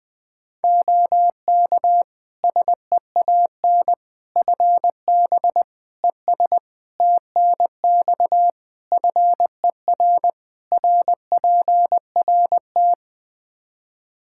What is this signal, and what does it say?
Morse code, heard as a keyed tone, about 20 wpm, 700 Hz. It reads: OK SEAN FB ES TNX FER RPRT
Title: Practice QSO - Exchange 2 - Info - 20wpm